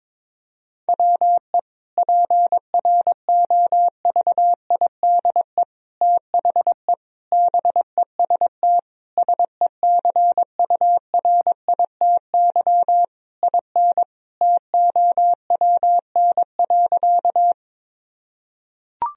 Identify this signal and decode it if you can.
Morse code, heard as a keyed tone, about 22 wpm, 700 Hz. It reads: WE PROVIDE THE BEST SECURITY IN TOWN.
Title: Sentences from Top 1000 Words - Review Part 4 - 22wpm